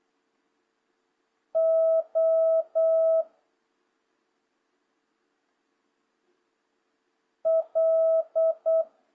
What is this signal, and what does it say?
Morse code, heard as a keyed tone, about 8 wpm, 650 Hz. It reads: O L
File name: jh8cbh / 2026cw